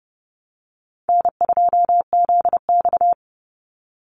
Morse code, heard as a keyed tone, about 30 wpm, 700 Hz. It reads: N2ZX
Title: Call Signs Repeated 3x - 30wpm